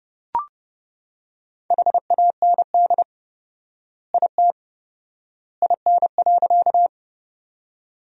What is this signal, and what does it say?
Morse code, heard as a keyed tone, about 30 wpm, 700 Hz. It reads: HAND IT IN.
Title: 4x Word-Spacing - Sentences from Top 200 Words - 30wpm